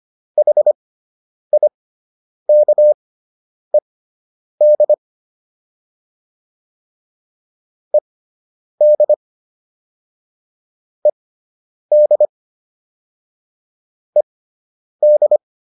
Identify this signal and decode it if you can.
Morse code, heard as a keyed tone, about 25 wpm, spaced as for 9 wpm, 600 Hz. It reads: HIKED ED ED ED